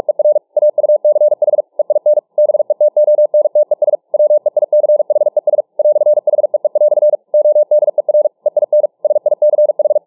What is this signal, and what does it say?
Morse code, heard as a keyed tone, about 45 wpm, 600 Hz. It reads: ER AUCH EIN BETONTES WEICHES "HEE" ODER EIN SICH